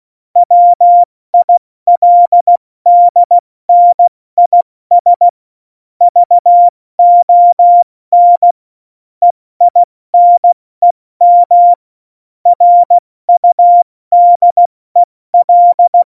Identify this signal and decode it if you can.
Morse code, heard as a keyed tone, about 16 wpm, 700 Hz. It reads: WILDNIS VON EINEM RUDEL
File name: Morsekurs / Witze / Witz 09. Oktober 2018, 70 F80